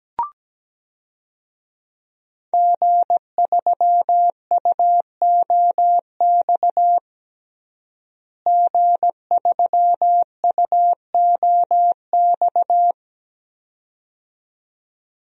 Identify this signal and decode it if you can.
Morse code, heard as a keyed tone, about 17 wpm, 700 Hz. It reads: G3UOX G3UOX
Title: DXCC-100 Call Signs - 5 Characters - Repeated 2x - 17wpm